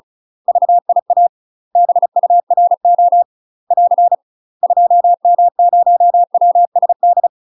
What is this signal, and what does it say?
Morse code, heard as a keyed tone, about 35 wpm, 700 Hz. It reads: VIA BURO <AR> 2M0WSD
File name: UK MASTER QSOs TRAINING / 35WPM / UK MASTER QSOs 0042